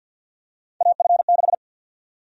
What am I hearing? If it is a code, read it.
Morse code, heard as a keyed tone, about 51 wpm, 700 Hz. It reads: AF6
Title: USA Call Sign Prefixes - 50wpm